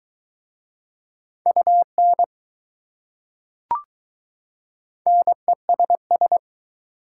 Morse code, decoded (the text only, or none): UN NESS